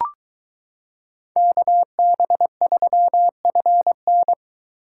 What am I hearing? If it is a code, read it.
Morse code, heard as a keyed tone, about 22 wpm, 700 Hz. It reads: KB3FN